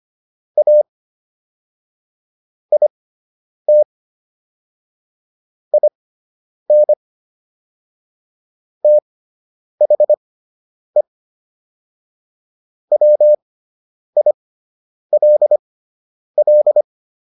Morse code, heard as a keyed tone, about 25 wpm, spaced as for 9 wpm, 600 Hz. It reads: A IT IN THE WILL